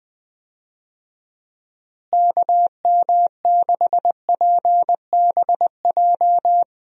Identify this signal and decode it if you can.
Morse code, heard as a keyed tone, about 21 wpm, 700 Hz. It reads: KM6PBJ
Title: USA Call Signs - 6 Characters - Repeated 2x - 20wpm